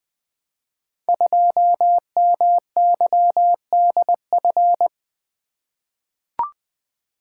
Morse code, heard as a keyed tone, about 20 wpm, 700 Hz. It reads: 2MYDF
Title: Code Groups - 5 Letter:Numbers - 20wpm